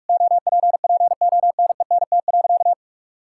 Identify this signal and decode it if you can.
Morse code, heard as a keyed tone, about 44 wpm, 700 Hz. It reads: OPPONENT.
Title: Pangram Challenge v2 - 45wpm